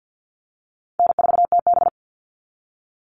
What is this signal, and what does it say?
Morse code, heard as a keyed tone, about 50 wpm, 700 Hz. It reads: N4TB
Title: Easy Call Signs - 50wpm